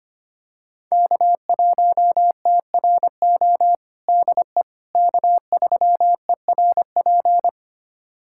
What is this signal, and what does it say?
Morse code, heard as a keyed tone, about 25 wpm, 700 Hz. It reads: K1TRO DE K3ERP
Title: USA Callsign DE Callsign 25wpm